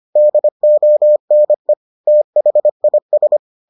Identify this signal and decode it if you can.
Morse code, heard as a keyed tone, about 25 wpm, 600 Hz. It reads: DONE THIS